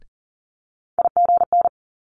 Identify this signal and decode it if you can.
Morse code, heard as a keyed tone, about 41 wpm, 700 Hz. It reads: IGN
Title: Multi-letter Phonemes Quick Fire 40wpm